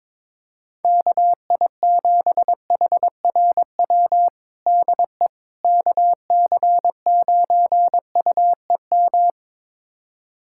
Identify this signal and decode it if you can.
Morse code, heard as a keyed tone, about 22 wpm, 700 Hz. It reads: KI7HRW DE KC9UEM
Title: USA Callsign DE Callsign - 6 Characters - 22wpm